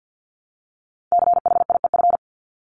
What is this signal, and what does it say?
Morse code, heard as a keyed tone, about 49 wpm, 700 Hz. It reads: CHIEF
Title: LY Suffix - Top 100 Words with Base Words 50wpm